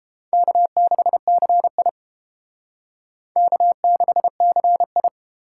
Morse code, heard as a keyed tone, about 29 wpm, 700 Hz. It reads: K6CI K6CI